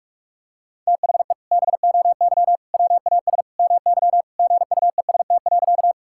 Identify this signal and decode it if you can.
Morse code, heard as a keyed tone, about 41 wpm, 700 Hz. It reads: THE BOY WAS MY GUEST.
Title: Sentences from Top 1600 Words 45wpm